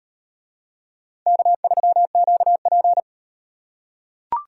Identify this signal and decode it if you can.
Morse code, heard as a keyed tone, about 35 wpm, 700 Hz. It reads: K3QP